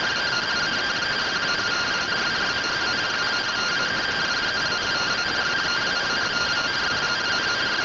Two-tone teleprinter signal, about 170 Hz shift.
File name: findings / CT AXIAL -111.gen-23 rtty rx